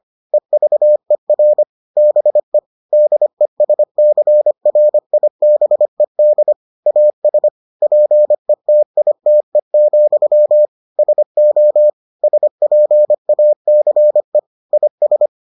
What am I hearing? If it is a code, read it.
Morse code, heard as a keyed tone, about 25 wpm, 600 Hz. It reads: EVER BE DESCRIBED AS PETITE, SO SPACE IS